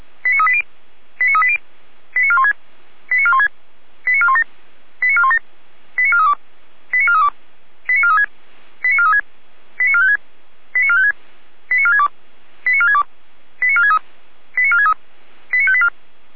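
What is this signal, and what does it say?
Five-tone selective call (ZVEI), heard as quick runs of five tones, tones of about 70 ms. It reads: 89380 89380 89416 89416 89417 89417 89432 89432 89436 89436 89456 89456 89462 89462 89463 89463 89584
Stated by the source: ZVEI five-tone call received by SDR.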